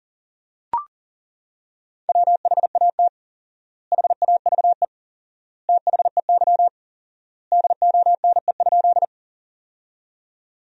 Morse code, heard as a keyed tone, about 37 wpm, 700 Hz. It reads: WHAT HAVE THEY DONE?
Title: 4x Word-Spacing - Sentences from Top 500 Words - 40wpm